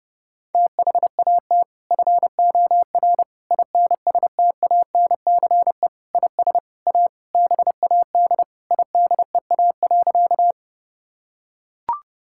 Morse code, heard as a keyed tone, about 30 wpm, 700 Hz. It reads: THAT FOR INSTANCE IS A BAD IDEA.
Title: Sentences from Top 1000 Words - Review Part 5 - 30wpm